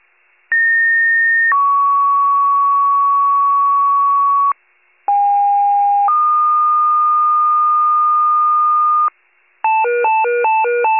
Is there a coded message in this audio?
Yes: two-tone page from 0.5 to 9.1 s, heard as a one-second tone, then a three-second tone, about 1850 then 1120 Hz, about 810 then 1220 Hz.